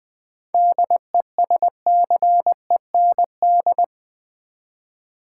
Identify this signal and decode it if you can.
Morse code, heard as a keyed tone, about 20 wpm, 700 Hz. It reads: DESCEND
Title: 7 Letter Words - First 3 Letters Given - 20wpm